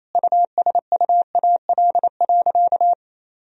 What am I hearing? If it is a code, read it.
Morse code, heard as a keyed tone, about 27 wpm, 700 Hz. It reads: USUAL.
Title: Sentences from Top 1000 Words - Review Part 2 - 28wpm